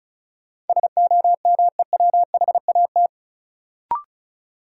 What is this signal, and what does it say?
Morse code, heard as a keyed tone, about 36 wpm, 700 Hz. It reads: SOMEWHAT
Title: Sentences from Top 1000 Words - Review Part 1 - New Word Drill - 35wpm